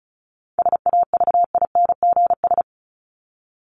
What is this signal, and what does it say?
Morse code, heard as a keyed tone, about 34 wpm, 700 Hz. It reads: SAVINGS
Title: Top 1001-1250 Words 35wpm